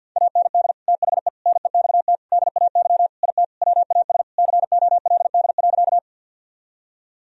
Morse code, heard as a keyed tone, about 51 wpm, 700 Hz. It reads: AND THE NEXT DAY IT WAS COLD.
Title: Sentences from Top 1000 Words - Review Part 1 - 50wpm